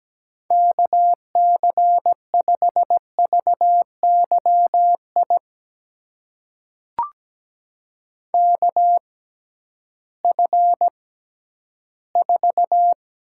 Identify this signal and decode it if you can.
Morse code, heard as a keyed tone, about 17 wpm, spaced as for 11 wpm, 700 Hz. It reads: KC5VYI KF4
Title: USA Call Signs - 6 Characters - 6x Character Spacing - 17wpm